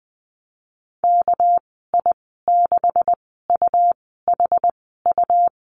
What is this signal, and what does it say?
Morse code, heard as a keyed tone, about 20 wpm, spaced as for 14 wpm, 700 Hz. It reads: KI6UHU
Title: 219 Lesson 35g - Number 0 - Call Signs 20wpm